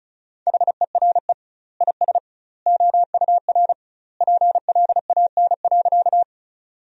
Morse code, heard as a keyed tone, about 36 wpm, 700 Hz. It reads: HERE IS OUR PLAN.
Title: 2x Word-Spacing - Sentences from Top 500 Words - 35wpm